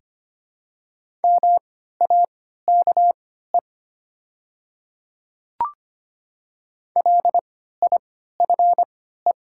Morse code, heard as a keyed tone, about 25 wpm, spaced as for 14 wpm, 700 Hz. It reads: MAKE LIFE
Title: 3x Character Spacing - Top 300 Words - 25wpm